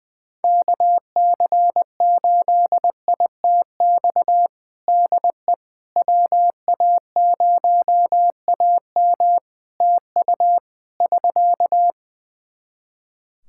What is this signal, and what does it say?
Morse code, heard as a keyed tone, about 20 wpm, 700 Hz. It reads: KC8ITX DE WA0AM TU <SK>
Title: Practice QSO - Exchange 4 - Ending - 20wpm